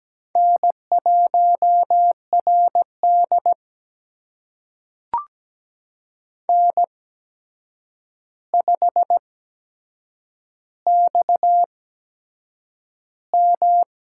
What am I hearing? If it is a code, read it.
Morse code, heard as a keyed tone, about 17 wpm, spaced as for 7 wpm, 700 Hz. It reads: N1RD N5XM
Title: USA Call Signs - 8x Character Spacing - 17wpm